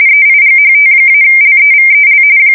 RTTY, about 45.45 baud. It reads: EIGHT-LEVEL
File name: fsk170 / teletype-corp